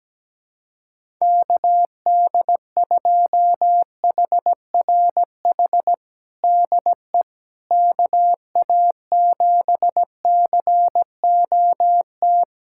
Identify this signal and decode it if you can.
Morse code, heard as a keyed tone, about 17 wpm, 700 Hz. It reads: KD2HRH DE KA7COT
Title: USA Callsign DE Callsign 17wpm